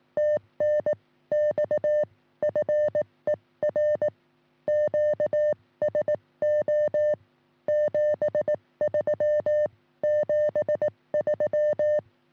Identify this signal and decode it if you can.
Morse code, heard as a keyed tone, about 18 wpm, spaced as for 16 wpm, 590 Hz. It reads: TNX FER QSO 73 73
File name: sm7lcb / 1941 sm0err 10g rs